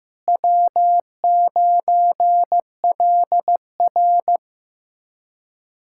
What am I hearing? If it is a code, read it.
Morse code, heard as a keyed tone, about 15 wpm, 700 Hz. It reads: W9LR
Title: K1USN SST QSO Practice - 15wpm